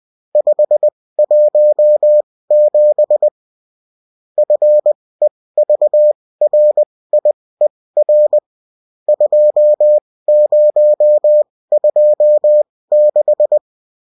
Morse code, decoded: 517 FEVRIER 2026